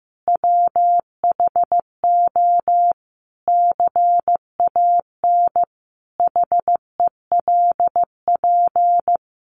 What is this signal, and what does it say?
Morse code, heard as a keyed tone, about 15 wpm, 700 Hz. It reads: WHO CAN HELP
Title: Sentences from Top 200 Words 15wpm